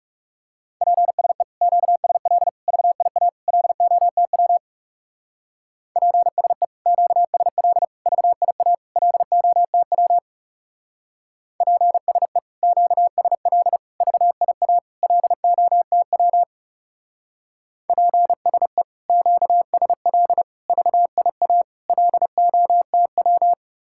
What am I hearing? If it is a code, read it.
Morse code, heard as a keyed tone, about 37 wpm, 700 Hz. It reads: PSE QSL VIA LOTW PSE QSL VIA LOTW PSE QSL VIA LOTW PSE QSL VIA LOTW
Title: Speed-Racing Common QSO Phrases - 30wpm